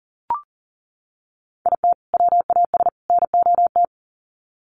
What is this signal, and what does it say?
Morse code, heard as a keyed tone, about 40 wpm, 700 Hz. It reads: IT WAS NOT
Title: Sets of 3 Words 40wpm